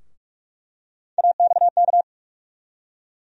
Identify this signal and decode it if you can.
Morse code, heard as a keyed tone, about 42 wpm, 700 Hz. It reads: AXK
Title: USA Call Sign Postfixes - 3 Letters 45wpm